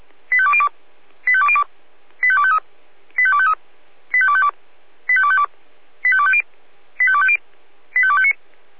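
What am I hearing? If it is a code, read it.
Five-tone selective call (ZVEI), heard as quick runs of five tones, tones of about 70 ms. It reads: 85202 85202 85253 85253 85252 85252 85250 85250 85259
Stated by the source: ZVEI five-tone selective call 85102 received on an SDR.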